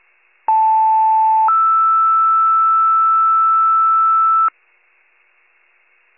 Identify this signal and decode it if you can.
Two-tone page, heard as a one-second tone, then a three-second tone, about 880 then 1320 Hz. It is heard from 0.5 to 4.5 s.